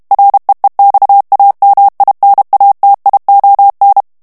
Morse code, heard as a keyed tone, about 32 wpm, 800 Hz. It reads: REEXAMINATION
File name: ANGLAIS texte clair 3 vitesse 30 mots minute